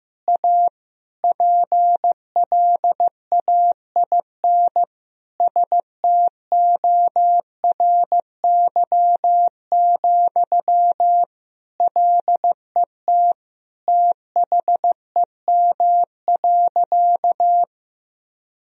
Morse code, heard as a keyed tone, about 16 wpm, 700 Hz. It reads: A PLAIN STORY, LET THEM.
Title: Sentences from Top 1000 Words - Review Part 2 - 15wpm